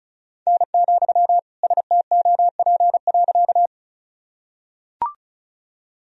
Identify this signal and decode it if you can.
Morse code, heard as a keyed tone, about 36 wpm, 700 Hz. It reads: N, STOP.
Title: Sentences from Top 300 Words and Abbreviations 35wpm